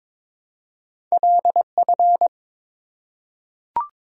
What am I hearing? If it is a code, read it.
Morse code, heard as a keyed tone, about 22 wpm, 700 Hz. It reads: LF
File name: practice / 22wpm / Two Letter-Number Combinations - 22wpm